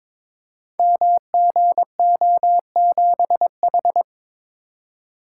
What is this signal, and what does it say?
Morse code, heard as a keyed tone, about 22 wpm, 700 Hz. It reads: MGO7H
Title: Code Groups - 5 Letter:Numbers - 22wpm